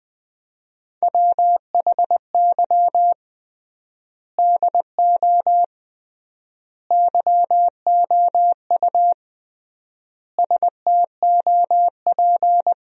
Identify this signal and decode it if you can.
Morse code, heard as a keyed tone, about 20 wpm, 700 Hz. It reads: WHY DO YOU STOP
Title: 3x Word-Spacing - Sentences from Top 300 Words - 20wpm